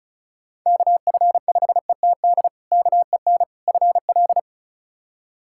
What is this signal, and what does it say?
Morse code, heard as a keyed tone, about 36 wpm, 700 Hz. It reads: KF5ETD KEN FL
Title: K1USN SST QSO Practice - 35wpm